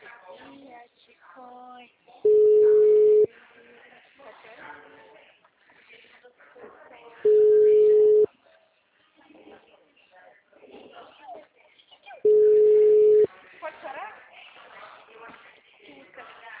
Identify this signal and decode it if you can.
Ringback tone (Europe, 425 Hz), with speech.